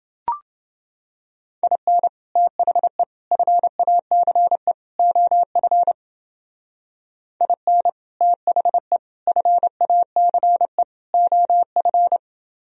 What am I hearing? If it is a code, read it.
Morse code, heard as a keyed tone, about 28 wpm, 700 Hz. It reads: IN THE FACE OF IN THE FACE OF